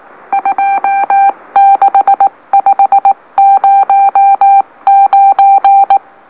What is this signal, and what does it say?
Morse code, heard as a keyed tone, about 19 wpm, spaced as for 16 wpm, 790 Hz. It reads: 26509
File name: M04- -2345utc-LOLO